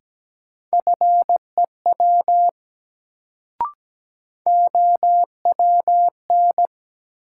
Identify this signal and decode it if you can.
Morse code, heard as a keyed tone, about 17 wpm, 700 Hz. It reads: FEW OWN
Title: Sentences from Top 300 Words - New Word Drill - 17wpm